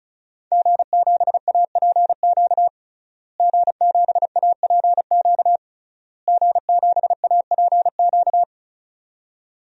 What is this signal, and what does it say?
Morse code, heard as a keyed tone, about 36 wpm, 700 Hz. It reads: G7APQ G7APQ G7APQ
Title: DXCC-100 Call Signs - 5 Characters - Repeated 3x - 35wpm